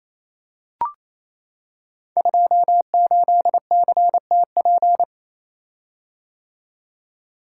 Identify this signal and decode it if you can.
Morse code, heard as a keyed tone, about 29 wpm, 690 Hz. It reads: 28CTP
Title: Code Groups - 5 Letter:Numbers - 28wpm